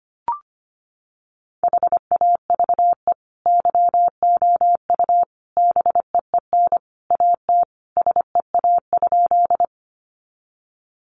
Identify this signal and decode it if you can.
Morse code, heard as a keyed tone, about 25 wpm, 700 Hz. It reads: HAVE YOU BEEN AT SEA?
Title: Sentences from Top 300 Words 25wpm